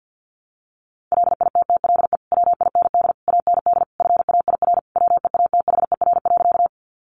Morse code, heard as a keyed tone, about 50 wpm, 700 Hz. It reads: LITTLE WIND AND FAIR WEATHER.
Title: Sentences from Top 1300 Words 50wpm